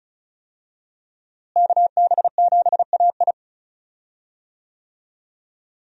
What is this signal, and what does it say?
Morse code, heard as a keyed tone, about 34 wpm, 700 Hz. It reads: KB7AI
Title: USA Call Signs - 5 Characters - Repeated 3x - 35wpm